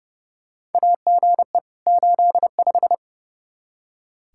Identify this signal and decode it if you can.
Morse code, heard as a keyed tone, about 30 wpm, 700 Hz. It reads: AGE 85
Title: Practice QSO - Exchange 3 - More Info - 30wpm